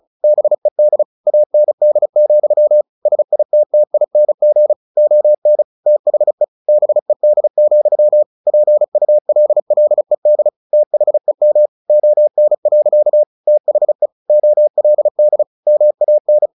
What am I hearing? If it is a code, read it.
Morse code, heard as a keyed tone, about 34 wpm, 600 Hz. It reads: BED AND, SITTING ON THE BED, PULLED THEM ON. THE OLD MAN
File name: the old man and the sea / the old man and the sea 1 35wpm